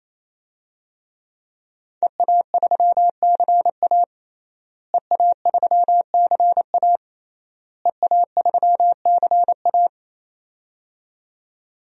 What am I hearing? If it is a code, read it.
Morse code, heard as a keyed tone, about 28 wpm, 700 Hz. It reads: EA3CA EA3CA EA3CA
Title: DXCC-100 Call Signs - 5 Characters - Repeated 3x - 28wpm